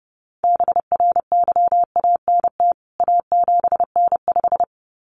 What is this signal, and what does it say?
Morse code, heard as a keyed tone, about 30 wpm, 700 Hz. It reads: BRYANT A7N5
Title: CWT Exchange 30wpm